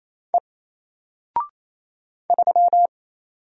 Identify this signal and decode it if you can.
Morse code, heard as a keyed tone, about 27 wpm, 700 Hz. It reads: E 3
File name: Single-Letter-Number-28wpm